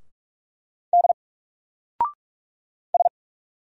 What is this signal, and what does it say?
Morse code, heard as a keyed tone, about 44 wpm, 700 Hz. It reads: D S
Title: Single Character (Letter:Number:Punctuation:Prosign) 45wpm